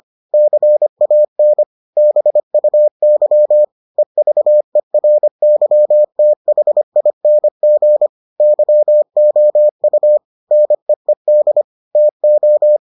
Morse code, heard as a keyed tone, about 25 wpm, 600 Hz. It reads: CAN BUY EVERYTHING YOU NEED TO